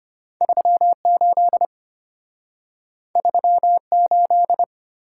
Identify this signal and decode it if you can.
Morse code, heard as a keyed tone, about 34 wpm, 700 Hz. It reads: 38 38